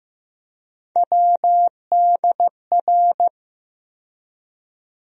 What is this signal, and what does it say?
Morse code, heard as a keyed tone, about 15 wpm, 700 Hz. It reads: WDR
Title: USA Call Sign Postfixes - 3 Letters 15wpm